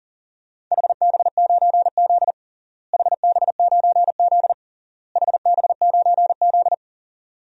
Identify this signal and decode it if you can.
Morse code, heard as a keyed tone, about 37 wpm, 700 Hz. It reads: HB9Z HB9Z HB9Z
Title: DXCC-100 Call Signs - 4 Characters - Repeated 3x - 40wpm